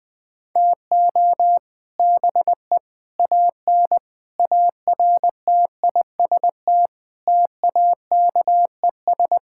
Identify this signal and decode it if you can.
Morse code, heard as a keyed tone, about 19 wpm, 700 Hz. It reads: TO BE AN ARTIST TAKES
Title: Sentences with 3-Word Phrases - 20wpm